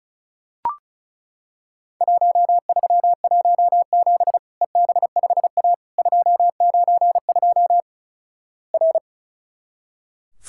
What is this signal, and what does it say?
Morse code, heard as a keyed tone, about 36 wpm, 700 Hz. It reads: 1317 EB5A 292 R
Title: Worked All Europe DX Contest - QTC Exchange - 35wpm